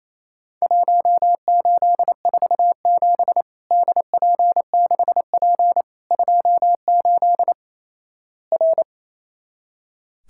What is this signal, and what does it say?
Morse code, heard as a keyed tone, about 28 wpm, 700 Hz. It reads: 1847 DP6P 28 R
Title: Worked All Europe DX Contest - QTC Exchange - 28wpm